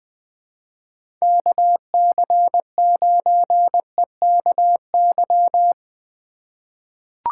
Morse code, heard as a keyed tone, about 21 wpm, 700 Hz. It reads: KC9EKY
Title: USA Call Signs - 6 Characters - 2x Character Spacing - 20wpm